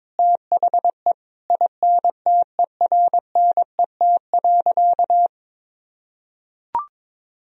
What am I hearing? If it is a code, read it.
Morse code, heard as a keyed tone, about 22 wpm, 700 Hz. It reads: THE INTERNET.
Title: Sentences from Top 1000 Words - Review Part 4 - 22wpm